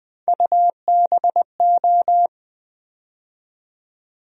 Morse code, heard as a keyed tone, about 20 wpm, 700 Hz. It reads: UBO
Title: USA Call Sign Postfixes - 3 Letters 20wpm